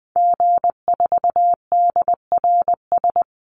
Morse code, heard as a keyed tone, about 20 wpm, 700 Hz. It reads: G4DRS
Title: Call Signs - 20wpm